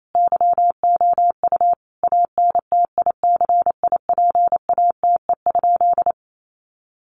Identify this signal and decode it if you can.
Morse code, heard as a keyed tone, about 28 wpm, 700 Hz. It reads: YOU ANTICIPATE?
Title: Sentences from Top 1500 Words 28wpm